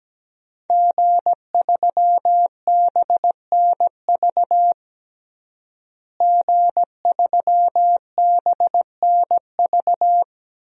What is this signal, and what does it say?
Morse code, heard as a keyed tone, about 17 wpm, 700 Hz. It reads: G3BNV G3BNV
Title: DXCC-100 Call Signs - 5 Characters - Repeated 3x - 17wpm